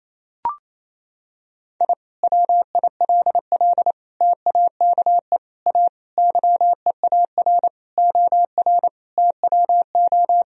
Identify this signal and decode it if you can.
Morse code, heard as a keyed tone, about 28 wpm, 700 Hz. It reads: I WILL TAKE A YEAR OR TWO
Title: Sentences from Top 1000 Words - Review Part 1 - 28wpm